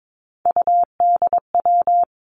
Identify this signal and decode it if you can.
Morse code, heard as a keyed tone, about 22 wpm, 700 Hz. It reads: UDW
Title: Three Challenging Letter-Number Combinations 22wpm